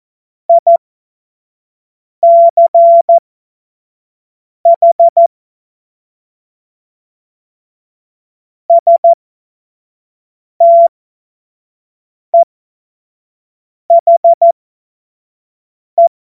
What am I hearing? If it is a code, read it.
Morse code, heard as a keyed tone, about 14 wpm, spaced as for 5 wpm, 680 Hz. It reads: ICH STEHE
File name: Morsekurs / Witze / Witz 10. August 2021, 25 F70